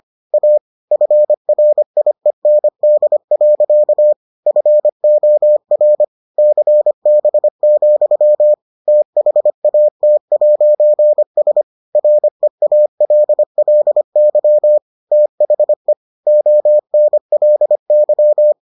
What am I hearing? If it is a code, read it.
Morse code, heard as a keyed tone, about 25 wpm, 600 Hz. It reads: A FRIEND. FOR CB, THAT'S REALLY THE ONLY